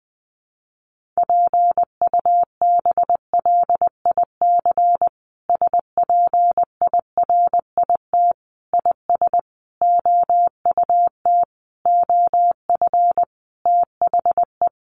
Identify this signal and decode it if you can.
Morse code, heard as a keyed tone, about 20 wpm, 700 Hz. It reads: PUBLIC SPIRIT IS OUT OF THE
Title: Sentences from Top 1400 Words 20wpm